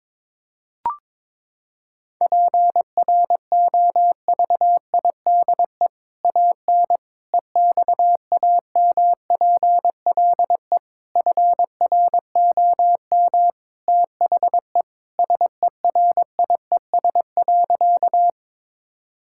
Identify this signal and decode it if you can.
Morse code, heard as a keyed tone, about 22 wpm, 700 Hz. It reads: PROVIDE AN EXAMPLE FROM THE SERIES.
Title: Sentences from Top 1000 Words - Review Part 5 - 22wpm